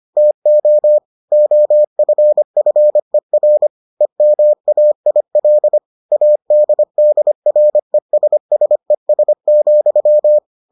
Morse code, heard as a keyed tone, about 25 wpm, 600 Hz. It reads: TO OFFER EMAIL ADDRESSES,